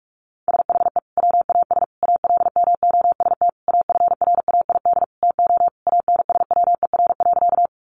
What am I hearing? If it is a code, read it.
Morse code, heard as a keyed tone, about 44 wpm, 700 Hz. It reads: SHE WAS ALMOST AFRAID TO ANSWER.
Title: Sentences from Top 1500 Words 45wpm